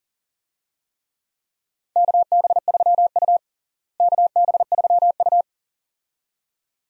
Morse code, heard as a keyed tone, about 37 wpm, 700 Hz. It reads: KB3U KB3U
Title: USA Call Signs - 4 Characters - Repeated 2x - 40wpm